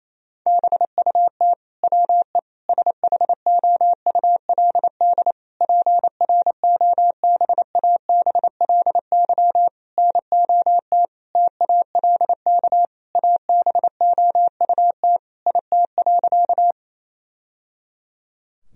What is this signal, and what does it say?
Morse code, heard as a keyed tone, about 28 wpm, 700 Hz. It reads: BUT WE SHOULD PROBABLY NOT TALK ABOUT IT.
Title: Sentences from Top 1000 Words - Review Part 4 - 28wpm